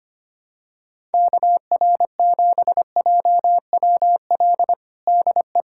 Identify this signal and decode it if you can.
Morse code, heard as a keyed tone, about 25 wpm, 700 Hz. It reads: KR7JWL DE
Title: USA Callsign DE Callsign - 6 Characters - 25wpm